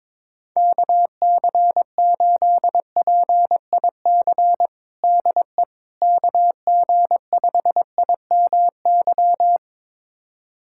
Morse code, heard as a keyed tone, about 21 wpm, 700 Hz. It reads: KC8PIC DE KG5IMY
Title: USA Callsign DE Callsign - 6 Characters - 22wpm